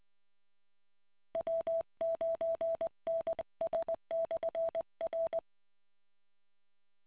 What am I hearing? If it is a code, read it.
Morse code, heard as a keyed tone, about 20 wpm, 660 Hz. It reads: W9NS/R